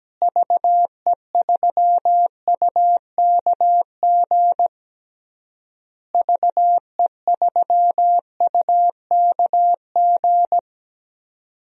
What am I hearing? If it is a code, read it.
Morse code, heard as a keyed tone, about 17 wpm, 700 Hz. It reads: VE3UKG VE3UKG
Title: DXCC-100 Call Signs - 6 Characters - Repeated 3x - 17wpm